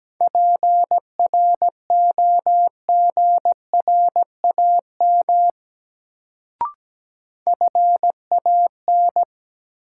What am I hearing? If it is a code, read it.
Morse code, heard as a keyed tone, about 17 wpm, 700 Hz. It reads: PROGRAM FAN
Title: Sentences from Top 1000 Words - Review Part 4 - New Word Drill - 17wpm